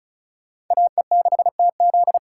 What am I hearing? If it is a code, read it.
Morse code, heard as a keyed tone, about 45 wpm, 700 Hz. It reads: AE6TZ